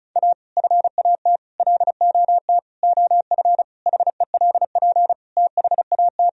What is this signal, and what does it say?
Morse code, heard as a keyed tone, about 36 wpm, 700 Hz. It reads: A FAT LOT OF HELP THAT
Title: Sentences from Top 1000 Words - Review Part 4 - 35wpm